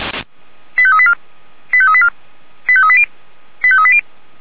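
Five-tone selective call (ZVEI), heard as quick runs of five tones, tones of about 70 ms. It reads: 85384 85384 85380 85380
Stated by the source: ZVEI five-tone selective call 89485, received on an SDR, from the feuerwehr-orlamuende fire brigade audio server.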